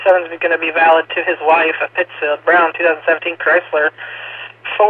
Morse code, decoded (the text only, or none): none